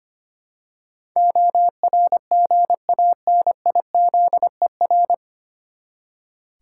Morse code, heard as a keyed tone, about 25 wpm, 700 Hz. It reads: ORGANIZER
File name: Nine Letter Words - 25wpm